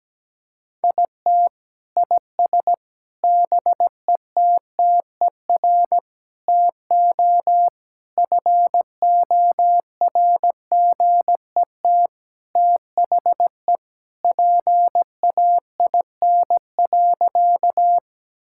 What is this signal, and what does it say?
Morse code, heard as a keyed tone, about 17 wpm, 700 Hz. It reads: IT IS BETTER TO FORGET THE PAIN.
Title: Sentences from Top 1600 Words 17wpm